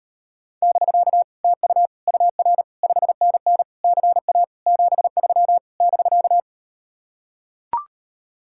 Morse code, heard as a keyed tone, about 39 wpm, 700 Hz. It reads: <BK> TU UR 5NN CA 73 <BK>